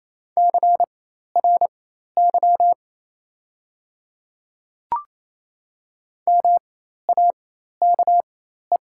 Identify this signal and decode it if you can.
Morse code, heard as a keyed tone, about 28 wpm, spaced as for 13 wpm, 700 Hz. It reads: CRY MAKE
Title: 4x Character Spacing - Top 500 Words - 28wpm